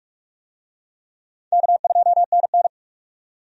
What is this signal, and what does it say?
Morse code, heard as a keyed tone, about 46 wpm, 700 Hz. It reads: K2NN
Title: USA Call Signs - 4 Characters - 45wpm